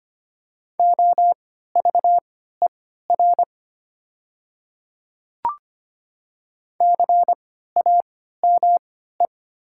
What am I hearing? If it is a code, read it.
Morse code, heard as a keyed tone, about 25 wpm, spaced as for 14 wpm, 700 Hz. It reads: OVER CAME
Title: 3x Character Spacing - Top 300 Words - 25wpm